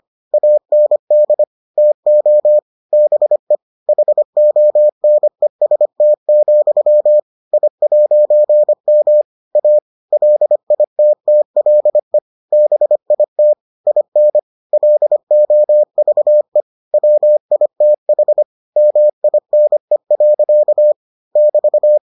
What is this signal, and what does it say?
Morse code, heard as a keyed tone, about 25 wpm, 600 Hz. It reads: AND TO BE HONEST, I'M A LITTLE BIT IN LOVE WITH MINE. <BT>